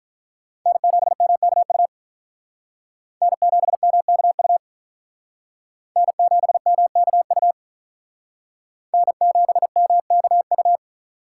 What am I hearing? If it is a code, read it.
Morse code, heard as a keyed tone, about 42 wpm, 700 Hz. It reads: N7MKU N7MKU N7MKU N7MKU
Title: Speed Racing - USA 1x3 Call Signs 35wpm